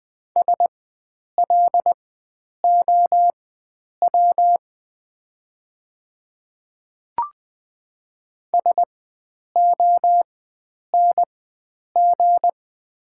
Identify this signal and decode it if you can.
Morse code, heard as a keyed tone, about 20 wpm, spaced as for 9 wpm, 700 Hz. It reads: SLOW SONG
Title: 4x Character Spacing - Top 500 Words - 20wpm